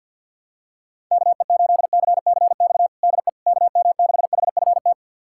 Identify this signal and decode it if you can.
Morse code, heard as a keyed tone, about 51 wpm, 700 Hz. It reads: KE8CCX DE KM6HFT
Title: USA Callsign DE Callsign - 6 Characters - 50wpm